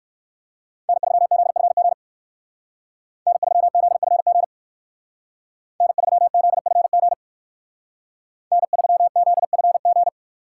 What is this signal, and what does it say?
Morse code, heard as a keyed tone, about 59 wpm, 700 Hz. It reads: N3ZFG N3ZFG N3ZFG N3ZFG